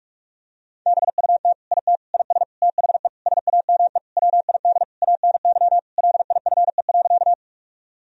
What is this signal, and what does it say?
Morse code, heard as a keyed tone, about 46 wpm, 700 Hz. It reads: BUT IT IS THE SAME WID ANY LIFE.
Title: Sentences from Top 300 Words and Abbreviations 45wpm